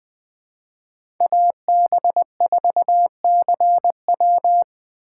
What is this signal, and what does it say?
Morse code, heard as a keyed tone, about 20 wpm, 700 Hz. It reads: AB4CW
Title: USA Call Signs - 5 Characters - 4x Character Spacing - 20wpm